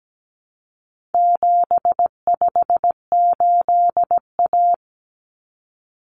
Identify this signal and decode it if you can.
Morse code, heard as a keyed tone, about 17 wpm, 700 Hz. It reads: 758A
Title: Common Cut Numbers - 4-Digit 17wpm